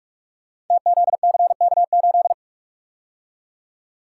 Morse code, heard as a keyed tone, about 44 wpm, 700 Hz. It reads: TZCK8
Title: Code Groups - 5 Letter:Numbers - 45wpm